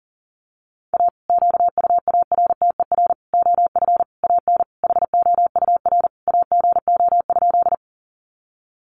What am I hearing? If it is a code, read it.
Morse code, heard as a keyed tone, about 40 wpm, 700 Hz. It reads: A QUARTER OF AN HOUR AGO?
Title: Sentences from Top 1300 Words 40wpm